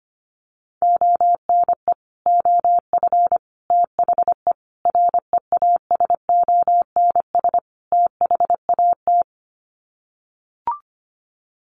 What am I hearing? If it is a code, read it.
Morse code, heard as a keyed tone, about 25 wpm, 700 Hz. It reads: ONE OF THE REASONS THAT
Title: Sets of 5 Words Encore 25wpm